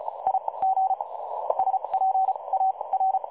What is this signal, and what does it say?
Morse code, heard as a keyed tone, about 35 wpm, spaced as for 27 wpm, 760 Hz. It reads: I G SPA R
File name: dlazaawansowanych / nagraniazpasma / websdr recording 3565.0kHz